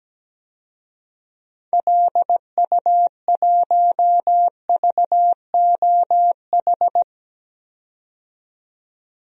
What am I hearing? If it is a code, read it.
Morse code, heard as a keyed tone, about 17 wpm, 700 Hz. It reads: LU1VOH
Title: DXCC-100 Call Signs - 6 Characters - 17wpm